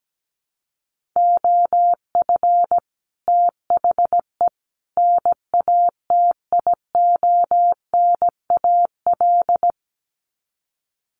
Morse code, heard as a keyed tone, about 17 wpm, 700 Hz. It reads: OF THE NATIONAL
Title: Sets of 3 Words 17wpm